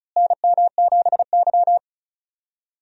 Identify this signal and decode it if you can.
Morse code, heard as a keyed tone, about 36 wpm, 700 Hz. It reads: NM7Y
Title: USA Call Signs - 4 Characters - Repeated 2x - 35wpm